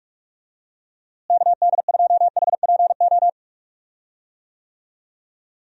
Morse code, heard as a keyed tone, about 41 wpm, 700 Hz. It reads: KD2HPO
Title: USA Call Signs - 6 Characters - Repeated 2x - 45wpm